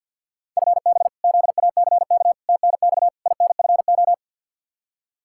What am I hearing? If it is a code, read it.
Morse code, heard as a keyed tone, about 48 wpm, 700 Hz. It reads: FB ZACK TNX INFO
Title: Practice QSO - Exchange 3 - More Info - 50wpm